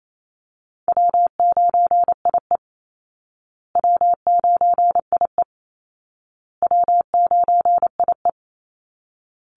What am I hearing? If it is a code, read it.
Morse code, heard as a keyed tone, about 28 wpm, 700 Hz. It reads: W9IE W9IE W9IE
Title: Call Signs Repeated 2x - 28wpm